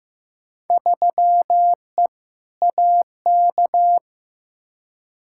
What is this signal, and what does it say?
Morse code, heard as a keyed tone, about 15 wpm, 700 Hz. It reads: 3E AK
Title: Field Day Exchanges Encore - 15wpm